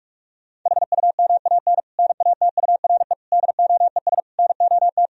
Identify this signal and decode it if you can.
Morse code, heard as a keyed tone, about 46 wpm, 700 Hz. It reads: HUMAN NATURE DOES NOT
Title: Sentences from Top 1000 Words - Review Part 3 - 45wpm